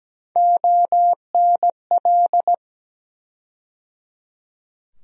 Morse code, heard as a keyed tone, about 17 wpm, 700 Hz. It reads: ONL
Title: USA Call Sign Postfixes - 3 Letters 17wpm